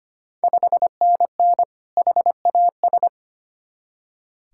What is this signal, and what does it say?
Morse code, heard as a keyed tone, about 25 wpm, 690 Hz. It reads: 5NN HAS
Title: Ontario QSO Party Exchange 25wpm